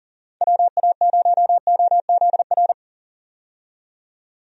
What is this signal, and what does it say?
Morse code, heard as a keyed tone, about 40 wpm, 700 Hz. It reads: WA0OZR
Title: USA Call Signs - 6 Characters - 40wpm